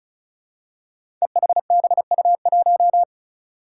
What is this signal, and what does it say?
Morse code, heard as a keyed tone, about 34 wpm, 700 Hz. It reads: EHBU1
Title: Five Challenging Letter-Number Combinations 35wpm